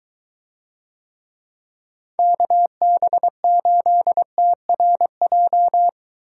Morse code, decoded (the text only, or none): KB8TRJ